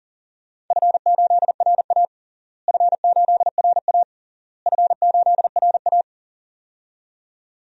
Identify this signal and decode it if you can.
Morse code, heard as a keyed tone, about 40 wpm, 700 Hz. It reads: F8RA F8RA F8RA
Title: DXCC-100 Call Signs - 4 Characters - Repeated 3x - 40wpm